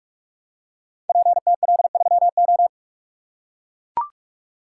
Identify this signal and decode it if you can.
Morse code, heard as a keyed tone, about 44 wpm, 700 Hz. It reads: PTL3O
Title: Code Groups - 5 Letter:Numbers - 45wpm